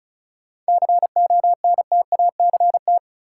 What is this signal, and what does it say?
Morse code, heard as a keyed tone, about 36 wpm, 700 Hz. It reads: CONTACT
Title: Sentences from Top 1000 Words - Review Part 7 - New Word Drill - 35wpm